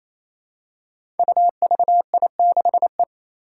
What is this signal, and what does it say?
Morse code, heard as a keyed tone, about 27 wpm, 700 Hz. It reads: UVI6E
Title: Code Groups - 5 Letter:Numbers - 28wpm